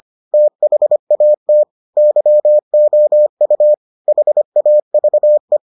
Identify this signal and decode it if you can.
Morse code, heard as a keyed tone, about 25 wpm, 600 Hz. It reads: THAT YOU HAVE